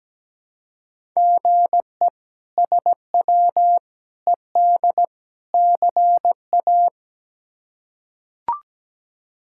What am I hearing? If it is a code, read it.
Morse code, heard as a keyed tone, about 17 wpm, 700 Hz. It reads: GE SW ED CA
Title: K1USN SST QSO Practice - 17wpm